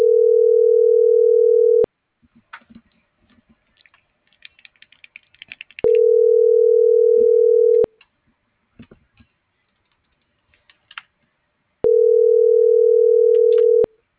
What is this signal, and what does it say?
Ringback tone (North America), 440 + 480 Hz.